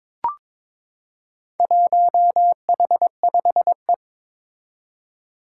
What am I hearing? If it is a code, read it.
Morse code, heard as a keyed tone, about 22 wpm, 700 Hz. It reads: 1H5E
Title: Four Challenging Letter-Number Combinations 22wpm